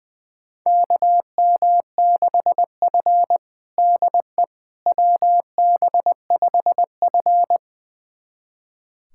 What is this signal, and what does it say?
Morse code, heard as a keyed tone, about 20 wpm, 700 Hz. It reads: KM6F DE WB5F
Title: USA Callsign DE Callsign - 4 Characters - 20wpm